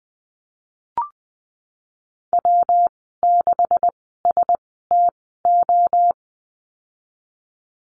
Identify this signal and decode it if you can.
Morse code, heard as a keyed tone, about 20 wpm, spaced as for 14 wpm, 700 Hz. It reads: W6STO
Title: 154 Lesson 25g - Letter p - Call Signs 20wpm